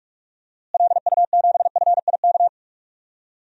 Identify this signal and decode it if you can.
Morse code, heard as a keyed tone, about 43 wpm, 700 Hz. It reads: LU7FIK
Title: DXCC-100 Call Signs - 6 Characters - Repeated 2x - 45wpm